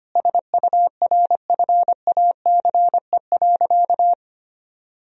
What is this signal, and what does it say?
Morse code, heard as a keyed tone, about 25 wpm, 700 Hz. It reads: SURFACE.
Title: Sentences from Top 1000 Words - Review Part 3 - 25wpm